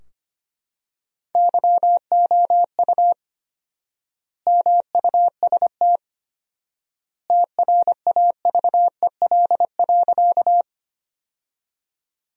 Morse code, heard as a keyed tone, about 25 wpm, 700 Hz. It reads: YOU MUST TRAVEL.
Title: 4x Word-Spacing - Sentences from Top 400 Words - 25wpm